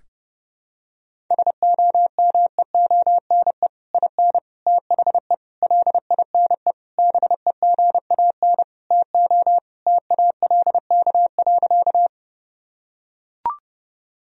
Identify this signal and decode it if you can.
Morse code, heard as a keyed tone, about 32 wpm, 700 Hz. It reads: SOMEONE IN THE LINE BEGAN TO TALK.
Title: Sentences from Top 1000 Words - Review Part 1 - 30wpm